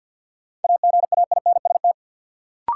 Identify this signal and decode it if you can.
Morse code, heard as a keyed tone, about 53 wpm, 700 Hz. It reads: AGAINST
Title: Sets of 2 Words - New Word Drill - 50wpm